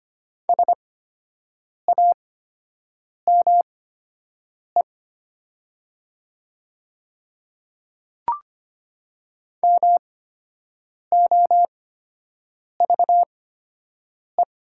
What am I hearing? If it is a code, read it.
Morse code, heard as a keyed tone, about 25 wpm, spaced as for 7 wpm, 700 Hz. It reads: SAME MOVE
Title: 8x Character Spacing - Top 200 Words - 25wpm